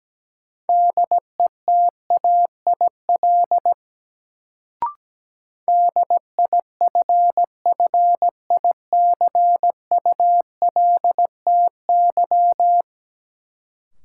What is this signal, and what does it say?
Morse code, heard as a keyed tone, about 17 wpm, 700 Hz. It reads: DETAIL DIFFICULTY
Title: Sentences from Top 1500 Words - New Word Drill - 17wpm